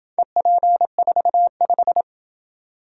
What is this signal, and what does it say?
Morse code, heard as a keyed tone, about 27 wpm, 700 Hz. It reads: EP45